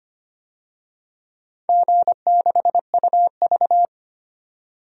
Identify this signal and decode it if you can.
Morse code, heard as a keyed tone, about 25 wpm, 700 Hz. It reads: G6UV
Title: DXCC-100 Call Signs - 4 Characters - 25wpm